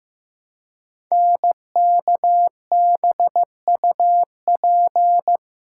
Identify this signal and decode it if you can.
Morse code, heard as a keyed tone, about 15 wpm, 700 Hz. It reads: NKBUP
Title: Code Groups - 5 Letters - 15wpm